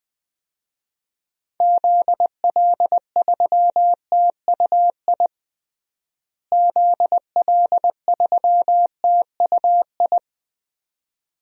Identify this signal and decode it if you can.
Morse code, heard as a keyed tone, about 20 wpm, 700 Hz. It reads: ZL3TUI ZL3TUI
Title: DXCC-100 Call Signs - 6 Characters - Repeated 2x - 20wpm